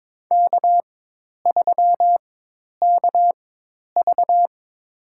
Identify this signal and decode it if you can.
Morse code, heard as a keyed tone, about 22 wpm, spaced as for 10 wpm, 700 Hz. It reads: K3KV
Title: USA Call Signs - 4 Characters - 4x Character Spacing - 22wpm